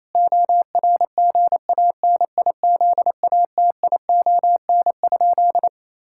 Morse code, heard as a keyed tone, about 28 wpm, 700 Hz. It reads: ORGANIZATION?
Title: Sentences from Top 1000 Words - Review Part 5 - 28wpm